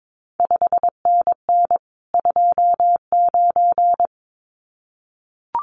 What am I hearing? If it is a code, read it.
Morse code, heard as a keyed tone, about 22 wpm, 700 Hz. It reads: 5NN 29
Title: CQ WW DX Contest Exchange Encore - 22wpm